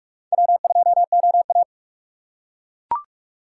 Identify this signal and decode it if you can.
Morse code, heard as a keyed tone, about 40 wpm, 700 Hz. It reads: W2OA